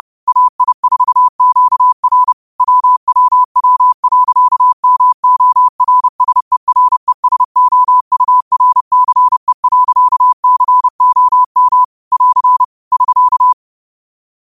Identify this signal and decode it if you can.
Morse code, heard as a keyed tone, about 30 wpm, 1000 Hz. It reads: AI4QR WWW.MORSERESOURCE.COM <AR> <SK>
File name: qotdfiles / 30wpm